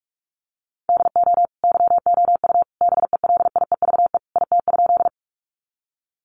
Morse code, heard as a keyed tone, about 44 wpm, 700 Hz. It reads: DO YOU BELIEVE IT?
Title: Sentences from Top 1000 Words 45wpm